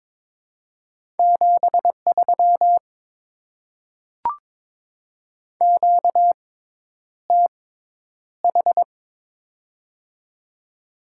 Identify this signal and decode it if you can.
Morse code, heard as a keyed tone, about 23 wpm, spaced as for 10 wpm, 700 Hz. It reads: 73 QTH
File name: Top 20 QSO Elements - 6x Character Spacing - 22wpm